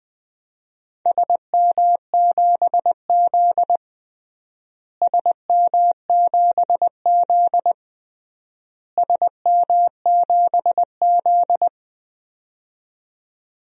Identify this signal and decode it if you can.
Morse code, heard as a keyed tone, about 21 wpm, 700 Hz. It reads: SM7Z SM7Z SM7Z
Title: DXCC-100 Call Signs - 4 Characters - Repeated 3x - 20wpm